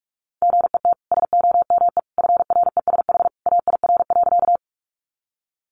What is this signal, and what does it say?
Morse code, heard as a keyed tone, about 44 wpm, 700 Hz. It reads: GET SOME FRESH AIR.
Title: Sentences from Top 1200 Words 45wpm